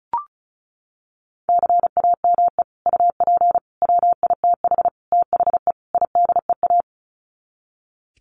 Morse code, heard as a keyed tone, about 34 wpm, 700 Hz. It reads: CAME UP WITH THE IDEA
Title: Sets of 5 Words Encore2 35wpm